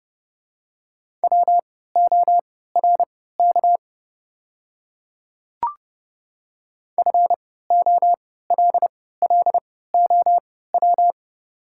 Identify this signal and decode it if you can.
Morse code, heard as a keyed tone, about 30 wpm, spaced as for 17 wpm, 700 Hz. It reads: WORK FOLLOW
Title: 3x Character Spacing - Top 300 Words - 30wpm